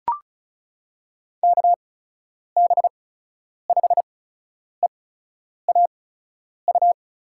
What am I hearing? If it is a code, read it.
Morse code, heard as a keyed tone, about 36 wpm, spaced as for 10 wpm, 700 Hz. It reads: KB5EAU
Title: USA Call Signs - 6 Characters - 8x Character Spacing - 35wpm